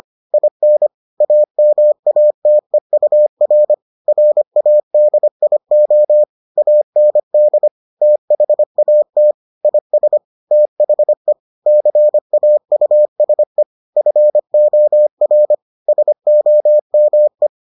Morse code, heard as a keyed tone, about 25 wpm, 600 Hz. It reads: IN AMATEUR RADIO AND THAT IS THE CAUSE FOR SOME